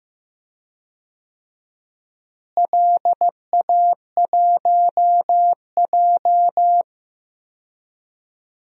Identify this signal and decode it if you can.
Morse code, heard as a keyed tone, about 15 wpm, 700 Hz. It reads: LA1J
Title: DXCC-100 Call Signs - 4 Characters - Repeated 3x - 15wpm